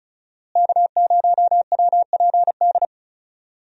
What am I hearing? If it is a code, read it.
Morse code, heard as a keyed tone, about 34 wpm, 700 Hz. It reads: K0WPD
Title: USA Call Signs - 5 Characters - Repeated 2x - 35wpm